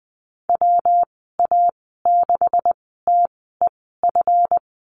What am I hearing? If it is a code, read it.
Morse code, heard as a keyed tone, about 20 wpm, spaced as for 14 wpm, 700 Hz. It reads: WA6TEF
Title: 147 Lesson 24g - Letter y - Call Signs 20wpm